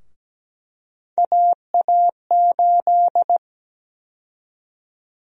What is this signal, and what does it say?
Morse code, heard as a keyed tone, about 17 wpm, 700 Hz. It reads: AA8
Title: USA Call Sign Prefixes - 17wpm